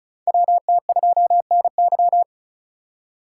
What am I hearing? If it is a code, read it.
Morse code, heard as a keyed tone, about 36 wpm, 700 Hz. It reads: WT2NY
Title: USA Call Signs - 5 Characters - 2x Character Spacing - 35wpm